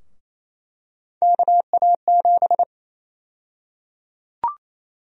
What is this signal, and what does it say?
Morse code, heard as a keyed tone, about 28 wpm, 700 Hz. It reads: KA7
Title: USA Call Sign Prefixes - Repeated 2x - 28wpm